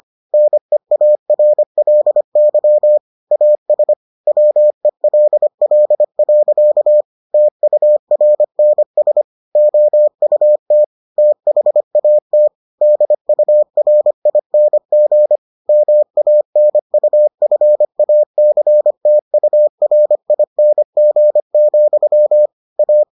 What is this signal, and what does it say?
Morse code, heard as a keyed tone, about 25 wpm, 600 Hz. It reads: NEARLY AS WELL. TURNS OUT THAT DURING MANUFACTURING, A